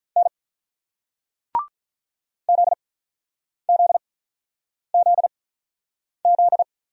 Morse code, heard as a keyed tone, about 45 wpm, spaced as for 9 wpm, 700 Hz. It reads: N ZZZZ